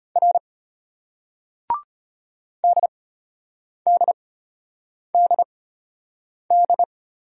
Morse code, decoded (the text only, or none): R DDDD